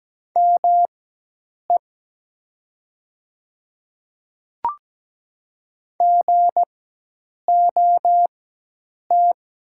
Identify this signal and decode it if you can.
Morse code, heard as a keyed tone, about 17 wpm, spaced as for 8 wpm, 700 Hz. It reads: ME GOT
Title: 4x Character Spacing - Top 300 Words - 17wpm